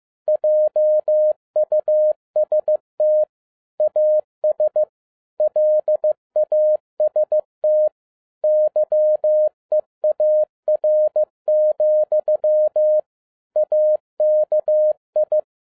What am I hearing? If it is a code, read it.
Morse code, heard as a keyed tone, about 15 wpm, 610 Hz. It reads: JUST AS LAST YEAR, AKI
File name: 2012-01-mp3 / n17-SUMMARY OF FEA NET IN 2011-15wpm